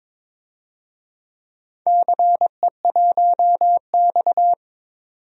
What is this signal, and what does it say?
Morse code, heard as a keyed tone, about 22 wpm, 700 Hz. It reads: CE1X
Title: DXCC-100 Call Signs - 4 Characters - 22wpm